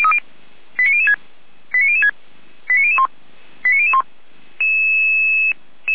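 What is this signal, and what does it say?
Five-tone selective call (ZVEI), heard as quick runs of five tones, tones of about 70 ms. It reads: 89006 89006 89001 89001
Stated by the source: ZVEI five-tone selective call received via SDR, Feuerwehr Orlamünde.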